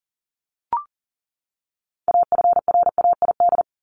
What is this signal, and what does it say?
Morse code, heard as a keyed tone, about 40 wpm, 700 Hz. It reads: AFRAID
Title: Top 1251-1500 Words 40wpm